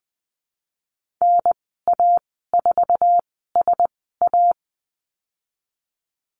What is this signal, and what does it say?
Morse code, heard as a keyed tone, about 20 wpm, spaced as for 15 wpm, 700 Hz. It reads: NA4SA
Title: 076 Lesson 13g - Letter L - Call Signs 20wpm